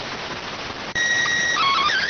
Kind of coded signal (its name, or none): SSTV (slow-scan television)